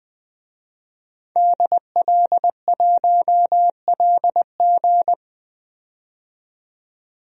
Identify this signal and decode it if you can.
Morse code, heard as a keyed tone, about 21 wpm, 700 Hz. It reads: DL1LG
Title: DXCC-100 Call Signs - 5 Characters - 20wpm